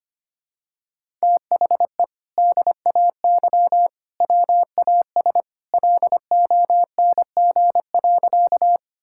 Morse code, heard as a keyed tone, about 25 wpm, 700 Hz. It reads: THE DAY WAS LONG.
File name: practice / 25wpm / Sentences with 2-Word Phrases - 25wpm